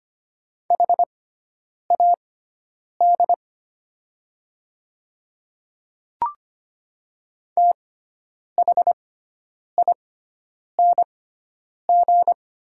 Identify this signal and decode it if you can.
Morse code, heard as a keyed tone, about 25 wpm, spaced as for 9 wpm, 700 Hz. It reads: HAD THING
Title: 6x Character Spacing - Top 100 Words - 25wpm